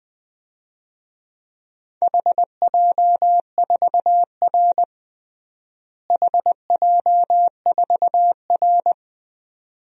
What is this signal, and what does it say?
Morse code, heard as a keyed tone, about 20 wpm, 700 Hz. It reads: HJ4R HJ4R
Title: DXCC-100 Call Signs - 4 Characters - Repeated 2x - 20wpm